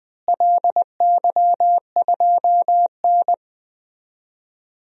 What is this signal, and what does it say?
Morse code, heard as a keyed tone, about 20 wpm, 700 Hz. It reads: LY2N
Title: DXCC-100 Call Signs - 4 Characters - Repeated 2x - 20wpm